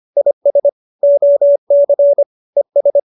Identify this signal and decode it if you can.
Morse code, heard as a keyed tone, about 25 wpm, 570 Hz. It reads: IS OC ES
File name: qso201b 25